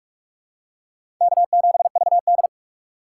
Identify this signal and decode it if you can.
Morse code, heard as a keyed tone, about 46 wpm, 700 Hz. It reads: K7VD
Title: USA Call Signs - 4 Characters - 3x Character Spacing - 45wpm